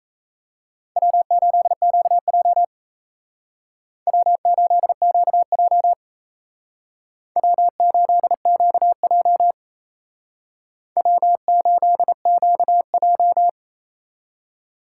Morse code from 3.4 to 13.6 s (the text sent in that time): W8QJ W8QJ W8QJ